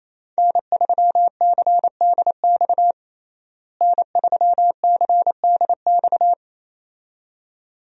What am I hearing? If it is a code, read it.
Morse code, heard as a keyed tone, about 28 wpm, 700 Hz. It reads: N3CDX N3CDX
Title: USA Call Signs - 5 Characters - Repeated 2x - 28wpm